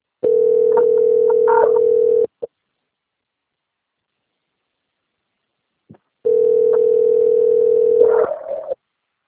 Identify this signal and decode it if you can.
Ringback tone (North America), 440 + 480 Hz.
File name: out- - - - - .44653